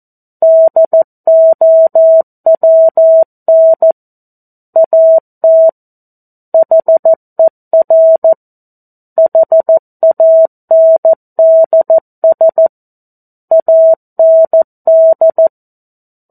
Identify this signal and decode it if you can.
Morse code, heard as a keyed tone, about 14 wpm, 650 Hz. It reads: DOWN AT HER HANDS AND
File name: 02-Chapter-02 14 WPM